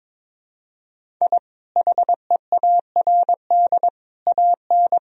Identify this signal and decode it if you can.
Morse code, heard as a keyed tone, about 22 wpm, 700 Hz. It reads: I HEARD AN
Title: Sentences from Top 1000 Words - Review Part 4 - 22wpm